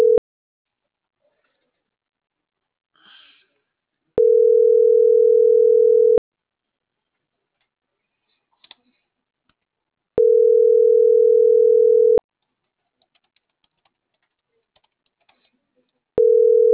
Ringback tone (North America), 440 + 480 Hz.